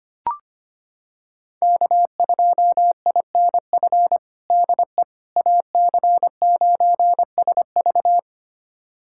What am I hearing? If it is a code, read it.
Morse code, heard as a keyed tone, about 25 wpm, 700 Hz. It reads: K2INF DE AC9SV
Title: USA Callsign DE Callsign - 5 Characters - 25wpm